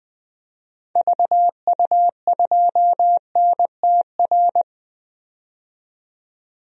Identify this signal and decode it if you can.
Morse code, heard as a keyed tone, about 20 wpm, 700 Hz. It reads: VU2NTR
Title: DXCC-100 Call Signs - 6 Characters - 20wpm